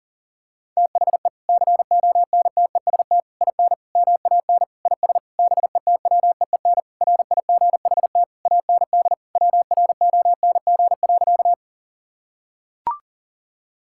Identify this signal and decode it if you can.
Morse code, heard as a keyed tone, about 37 wpm, 700 Hz. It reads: THE CONTEST IN MAN IS BETWEEN RIGHT AND WRONG.
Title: Sentences from Top 1600 Words 40wpm